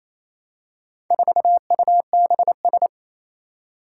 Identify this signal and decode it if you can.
Morse code, heard as a keyed tone, about 27 wpm, 700 Hz. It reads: 4UBS
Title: Four Challenging Letter-Number Combinations 28wpm